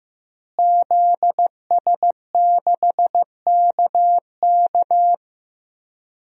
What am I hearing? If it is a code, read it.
Morse code, heard as a keyed tone, about 15 wpm, 700 Hz. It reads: ZS6KK
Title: DXCC-100 Call Signs - 5 Characters - Repeated 3x - 15wpm